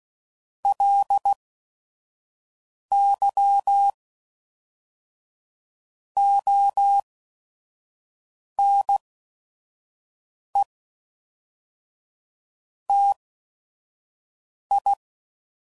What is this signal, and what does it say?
Morse code, heard as a keyed tone, about 16 wpm, spaced as for 5 wpm, 800 Hz. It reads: LY ONE TI
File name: CW Code Groups @5WPM